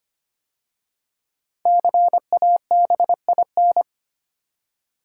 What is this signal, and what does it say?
Morse code, heard as a keyed tone, about 25 wpm, 700 Hz. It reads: CABIN
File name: 5 Letter Words - First 3 Letters Given - 25wpm